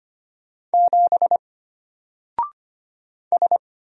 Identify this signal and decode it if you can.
Morse code, heard as a keyed tone, about 25 wpm, 700 Hz. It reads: 7 S
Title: Single Letter-Number Alt 25wpm